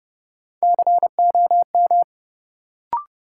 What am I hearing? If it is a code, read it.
Morse code, heard as a keyed tone, about 32 wpm, 700 Hz. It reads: COM
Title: Sentences from Top 1300 Words - New Word Drill - 30wpm